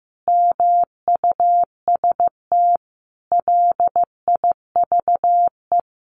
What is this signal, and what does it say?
Morse code, heard as a keyed tone, about 15 wpm, 700 Hz. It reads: MUST LIVE
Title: Sentences from Top 1400 Words 15wpm